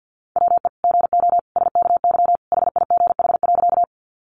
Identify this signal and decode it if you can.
Morse code, heard as a keyed tone, about 50 wpm, 700 Hz. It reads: WE GO SKY HIGH.
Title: Sentences from Top 1200 Words 50wpm